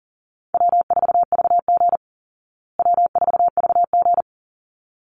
Morse code, heard as a keyed tone, about 40 wpm, 700 Hz. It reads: W4VG W4VG
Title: Call Signs Repeated 3x - 40wpm